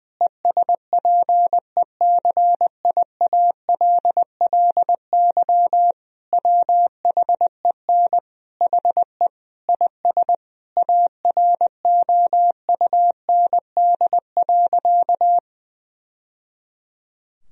Morse code, heard as a keyed tone, about 20 wpm, 700 Hz. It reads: ESPECIALLY WHEN HE IS AROUND.
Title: Sentences from Top 1000 Words - Review Part 4 - 20wpm